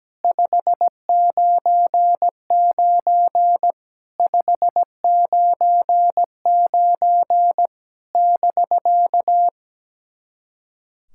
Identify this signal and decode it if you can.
Morse code, heard as a keyed tone, about 17 wpm, 700 Hz. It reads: 599 599 <BK>
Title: SOTA QSO Practice 17wpm